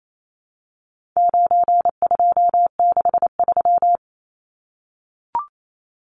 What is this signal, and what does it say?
Morse code, heard as a keyed tone, about 28 wpm, 700 Hz. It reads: 9263
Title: Numbers - 4-Digit 28wpm